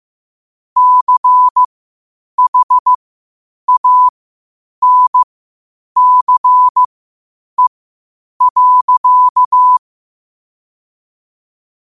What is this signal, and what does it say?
Morse code, heard as a keyed tone, about 15 wpm, spaced as for 8 wpm, 1000 Hz. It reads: CHANCE.
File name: qotdfiles / 8wpm